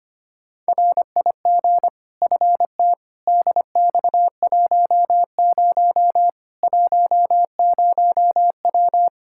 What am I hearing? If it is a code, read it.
Morse code, heard as a keyed tone, about 25 wpm, 700 Hz. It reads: RIG FT DX10 10W
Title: K1USN SST QSO Practice 25wpm